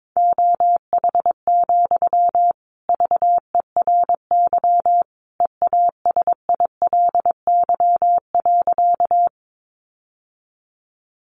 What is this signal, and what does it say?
Morse code, heard as a keyed tone, about 22 wpm, 700 Hz. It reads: OH, VERY EASILY.
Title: Sentences from Top 900 Words 22wpm